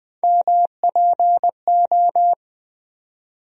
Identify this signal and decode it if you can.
Morse code, heard as a keyed tone, about 20 wpm, 700 Hz. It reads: MPO
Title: USA Call Sign Postfixes - 3 Letters 20wpm